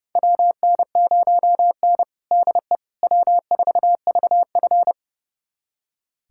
Morse code, heard as a keyed tone, about 30 wpm, 700 Hz. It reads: WN0N DE W4VF
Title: USA Callsign DE Callsign - 4 Characters - 30wpm